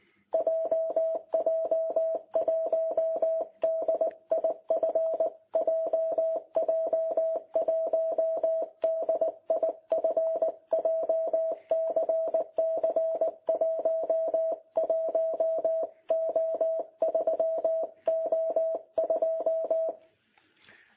Morse code, heard as a keyed tone, about 18 wpm, 660 Hz. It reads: JJ1DIF JJ1DIF JCC 11O3O2
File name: jh8cbh / 2021cw